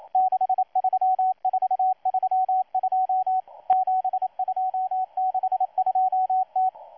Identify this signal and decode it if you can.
Morse code, heard as a keyed tone, about 28 wpm, 750 Hz, with radio noise. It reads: 63432 7262T
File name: M12-2016-07-07-1310utc-13926khz-error-restart